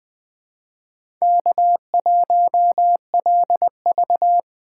Morse code, heard as a keyed tone, about 20 wpm, 700 Hz. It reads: K1LV
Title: Speed Racing - USA 1x2 Call Signs 17wpm